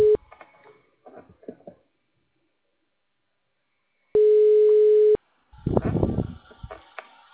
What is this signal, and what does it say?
Ringback tone (Europe, 425 Hz).